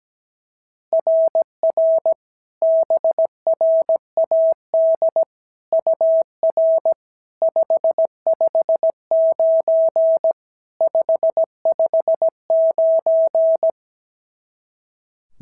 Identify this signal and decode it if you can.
Morse code, heard as a keyed tone, about 16 wpm, 650 Hz. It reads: RR BRAD UR 559 559
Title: SOTA QSO Practice 17wpm